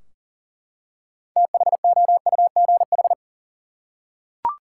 Morse code, heard as a keyed tone, about 38 wpm, 700 Hz. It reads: THOUGH
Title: Six Letter Words - 40wpm